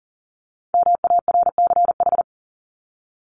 Morse code, heard as a keyed tone, about 40 wpm, 700 Hz. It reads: MARCH
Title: Top 1501-1750 Words 40wpm